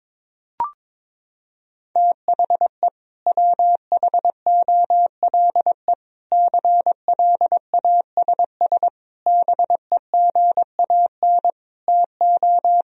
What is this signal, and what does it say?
Morse code, heard as a keyed tone, about 22 wpm, 700 Hz. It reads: THE WHOLE CLASS BEGAN TO
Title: Sentences from Top 1000 Words - Review Part 2 - 22wpm